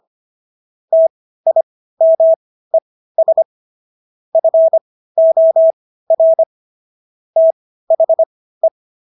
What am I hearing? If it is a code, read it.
Morse code, heard as a keyed tone, about 25 wpm, spaced as for 15 wpm, 650 Hz. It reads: TIMES FOR THE
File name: SS110 15